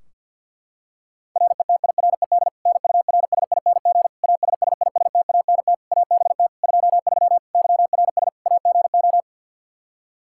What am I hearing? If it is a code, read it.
Morse code, heard as a keyed tone, about 51 wpm, 700 Hz. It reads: RETIRED NURSING ASSISTANT ABT 23 YRS AGO
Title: Practice QSO - Exchange 3 - More Info - 50wpm